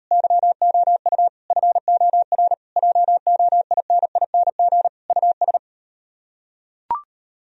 Morse code, heard as a keyed tone, about 38 wpm, 700 Hz. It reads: YOU FOR JOINING US